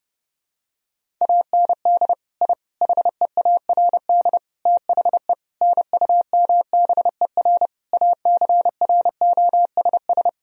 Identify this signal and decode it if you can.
Morse code, heard as a keyed tone, about 30 wpm, 700 Hz. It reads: AND I HEARD THE NUMBER ACROSS
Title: Sentences from Top 1000 Words - Review Part 1 - 30wpm